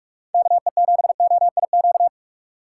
Morse code, heard as a keyed tone, about 46 wpm, 700 Hz. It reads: KE7OIQ
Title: USA Call Signs - 6 Characters - 2x Character Spacing - 45wpm